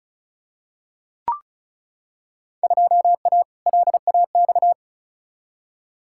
Morse code, heard as a keyed tone, about 36 wpm, 700 Hz. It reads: 2A LAX
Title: Field Day Exchanges - 35wpm